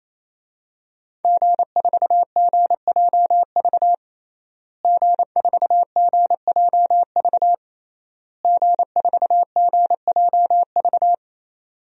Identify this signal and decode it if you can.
Morse code, heard as a keyed tone, about 28 wpm, 700 Hz. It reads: G4GJV G4GJV G4GJV
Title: DXCC-100 Call Signs - 5 Characters - Repeated 3x - 28wpm